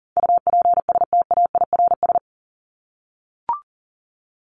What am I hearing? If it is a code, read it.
Morse code, heard as a keyed tone, about 40 wpm, 700 Hz. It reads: UPSTAIRS
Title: Top 1751-2000 Words 40wpm